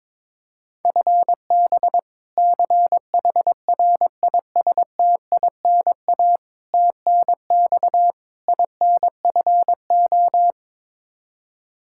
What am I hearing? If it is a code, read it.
Morse code, heard as a keyed tone, about 22 wpm, 700 Hz. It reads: FB CHRISTINA TNX INFO
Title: Practice QSO - Exchange 3 - More Info - 22wpm